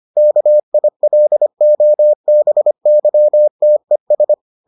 Morse code, heard as a keyed tone, about 25 wpm, 600 Hz. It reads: KILOBYTES